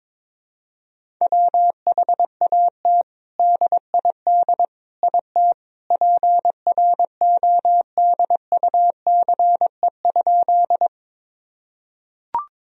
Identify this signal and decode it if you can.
Morse code, heard as a keyed tone, about 22 wpm, 700 Hz. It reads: WHAT DID IT PRODUCE?
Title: Sentences from Top 1000 Words - Review Part 3 - 22wpm